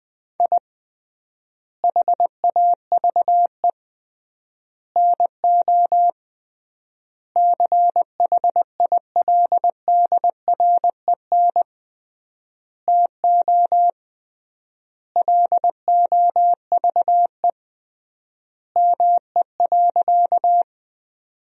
Morse code, heard as a keyed tone, about 20 wpm, 700 Hz. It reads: I HAVE NO CHILDREN TO LOVE ME.
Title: 3x Word-Spacing - Sentences from Top 400 Words - 20wpm